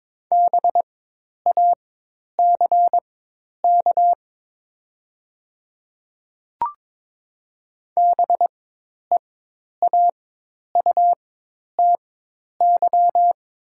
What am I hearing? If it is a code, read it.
Morse code, heard as a keyed tone, about 22 wpm, spaced as for 10 wpm, 700 Hz. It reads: BACK BEAUTY
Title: 4x Character Spacing - Top 500 Words - 22wpm